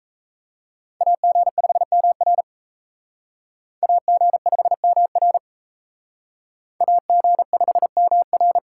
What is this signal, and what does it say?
Morse code, heard as a keyed tone, about 34 wpm, 700 Hz. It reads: AG5MR AG5MR AG5MR
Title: Speed Racing - USA 2x2 Call Signs 28wpm